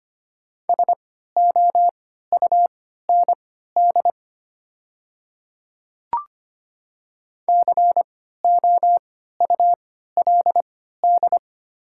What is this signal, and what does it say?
Morse code, heard as a keyed tone, about 25 wpm, spaced as for 14 wpm, 700 Hz. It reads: SOUND COULD
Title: 3x Character Spacing - Top 100 Words - 25wpm